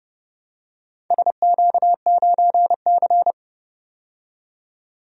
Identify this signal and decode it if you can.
Morse code, heard as a keyed tone, about 30 wpm, 700 Hz. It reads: SQ9C
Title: DXCC-100 Call Signs - 4 Characters - 30wpm